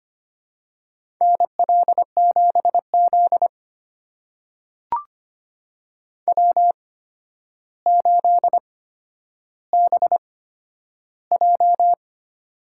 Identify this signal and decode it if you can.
Morse code, heard as a keyed tone, about 25 wpm, spaced as for 11 wpm, 700 Hz. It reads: NL7Z W8BJ
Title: USA Call Signs - 4 Characters - 8x Character Spacing - 25wpm